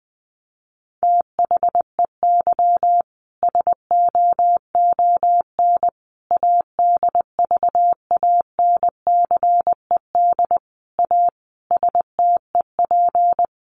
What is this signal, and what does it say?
Morse code, heard as a keyed tone, about 20 wpm, 700 Hz. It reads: THEY SOON ADVANCED A STEP
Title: Sentences from Top 1200 Words 20wpm